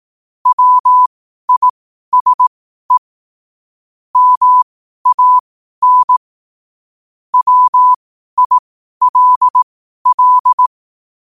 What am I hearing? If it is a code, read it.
Morse code, heard as a keyed tone, about 18 wpm, spaced as for 12 wpm, 1000 Hz. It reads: WISE MAN WILL
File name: qotdfiles / 12wpm